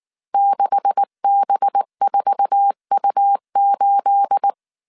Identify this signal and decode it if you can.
Morse code, heard as a keyed tone, about 19 wpm, 800 Hz. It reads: 6B4U8
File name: CW / DET / 1 9 1500C S A 18wpm 2459